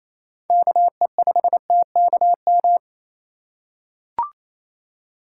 Morse code, heard as a keyed tone, about 27 wpm, 700 Hz. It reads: KE5TKM
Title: USA Call Signs - 6 Characters - 2x Character Spacing - 28wpm